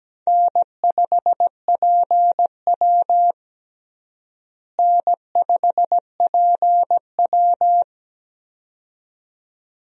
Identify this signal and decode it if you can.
Morse code, heard as a keyed tone, about 17 wpm, 700 Hz. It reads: N5PW N5PW
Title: USA Call Signs - 4 Characters - Repeated 2x - 17wpm